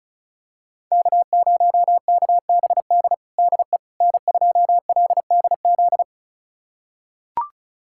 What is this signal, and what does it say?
Morse code, heard as a keyed tone, about 34 wpm, 700 Hz. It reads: K0KBD DE N2LDZ
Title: USA Callsign DE Callsign - 5 Characters - 35wpm